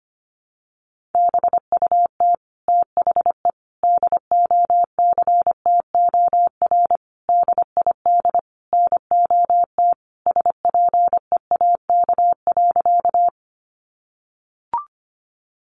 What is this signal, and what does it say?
Morse code, heard as a keyed tone, about 25 wpm, 700 Hz. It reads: BUT THE DOCTOR DID NOT SPEAK.
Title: Sentences from Top 1500 Words 25wpm